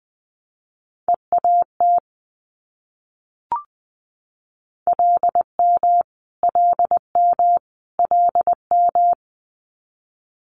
Morse code, heard as a keyed tone, about 20 wpm, 700 Hz. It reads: EAT LM LM LM
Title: Multi-letter Phonemes Repeated 3x 20wpm